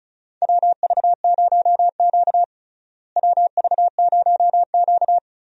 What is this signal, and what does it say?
Morse code, heard as a keyed tone, about 36 wpm, 700 Hz. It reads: WV0Q WV0Q
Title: USA Call Signs - 4 Characters - Repeated 3x - 35wpm